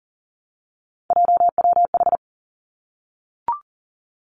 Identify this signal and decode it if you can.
Morse code, heard as a keyed tone, about 38 wpm, 690 Hz. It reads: JWH
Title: Three Challenging Letter-Number Combinations 40wpm